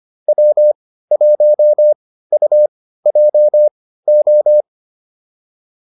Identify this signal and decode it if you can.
Morse code, heard as a keyed tone, about 25 wpm, spaced as for 15 wpm, 600 Hz. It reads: W1UJO
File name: QSO401 15